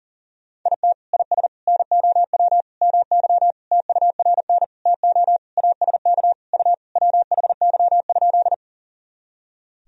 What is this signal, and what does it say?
Morse code, heard as a keyed tone, about 40 wpm, 700 Hz. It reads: IT IS NOW MY TURN TO ASK U WHY?
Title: Sentences from Top 200 Words and Abbreviations 40wpm